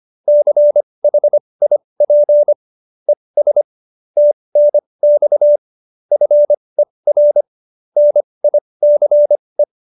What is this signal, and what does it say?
Morse code, heard as a keyed tone, about 25 wpm, spaced as for 20 wpm, 590 Hz. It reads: CHIP ES TNX FER NICE
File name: QSO316 20